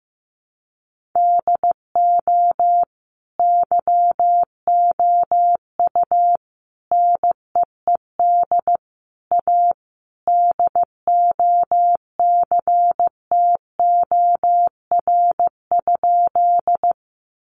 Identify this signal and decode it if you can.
Morse code, heard as a keyed tone, about 14 wpm, 700 Hz. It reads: DO YOU NEED A DOCTOR?
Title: Sentences from Top 1500 Words 15wpm